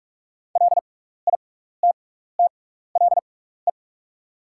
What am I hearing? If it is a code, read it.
Morse code, heard as a keyed tone, about 44 wpm, spaced as for 15 wpm, 690 Hz. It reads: LITTLE
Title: 6x Character Spacing - Top 200 Words - 45wpm